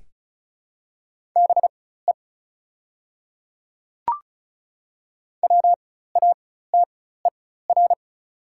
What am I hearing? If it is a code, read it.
Morse code, heard as a keyed tone, about 34 wpm, spaced as for 16 wpm, 690 Hz. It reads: BE WATER
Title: 4x Character Spacing - Top 200 Words - 35wpm